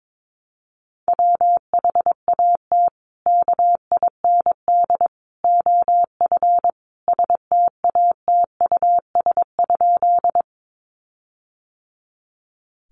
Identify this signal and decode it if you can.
Morse code, heard as a keyed tone, about 22 wpm, 700 Hz. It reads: WHAT KIND OF STATUS?
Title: Sentences from Top 1400 Words 22wpm